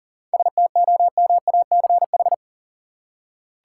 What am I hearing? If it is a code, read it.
Morse code, heard as a keyed tone, about 40 wpm, 700 Hz. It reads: STOMACH
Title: Seven Letter Words - 40wpm